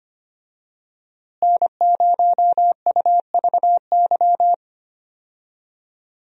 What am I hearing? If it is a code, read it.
Morse code, heard as a keyed tone, about 25 wpm, 700 Hz. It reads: N0UVY
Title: USA Call Signs - 5 Characters - Repeated 2x - 25wpm